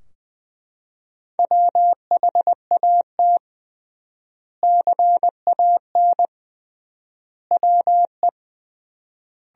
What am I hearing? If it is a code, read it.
Morse code, heard as a keyed tone, about 20 wpm, 700 Hz. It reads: WHAT CAN WE
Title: 3x Word-Spacing - Sentences from Top 100 Words - 20wpm